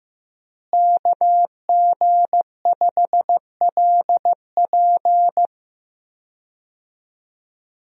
Morse code, heard as a keyed tone, about 15 wpm, 700 Hz. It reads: KG5LP
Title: USA Call Signs - 5 Characters - Repeated 3x - 15wpm